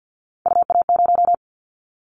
Morse code, heard as a keyed tone, about 50 wpm, 700 Hz. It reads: UA0
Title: Callsign Prefixes for Top 50 DXCC - 50wpm